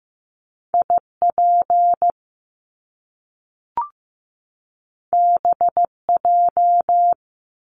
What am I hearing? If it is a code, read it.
Morse code, heard as a keyed tone, about 15 wpm, 700 Hz. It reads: IP BJ
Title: Top Two Letter Combinations in Words 15wpm